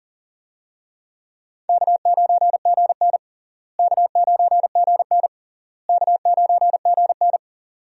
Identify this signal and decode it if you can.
Morse code, heard as a keyed tone, about 40 wpm, 700 Hz. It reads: K9GN K9GN K9GN
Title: USA Call Signs - 4 Characters - Repeated 3x - 40wpm